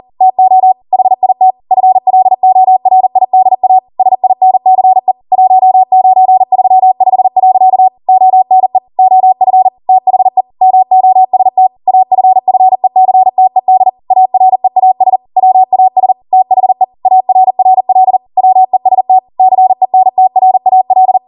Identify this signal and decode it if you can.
Morse code, heard as a keyed tone, about 40 wpm, 750 Hz. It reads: TO HIT FLORIDA SINCE 1935. ONE OF THE MOST AFFECTED AREAS WAS THE ARRL WEST CENTRAL